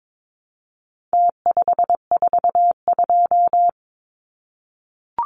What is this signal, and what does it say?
Morse code, heard as a keyed tone, about 22 wpm, 700 Hz. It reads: T542
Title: Numbers - Contest Style - 4-Digit 22wpm